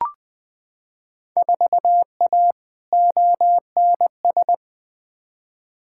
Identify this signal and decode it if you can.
Morse code, heard as a keyed tone, about 20 wpm, 700 Hz. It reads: 4A ONS
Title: Field Day Exchanges - 20wpm